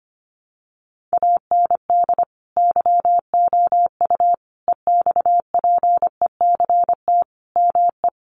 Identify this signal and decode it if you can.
Morse code, heard as a keyed tone, about 25 wpm, 700 Hz. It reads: AND YOU EXPECT ME
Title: Sentences from Top 1100 Words 25wpm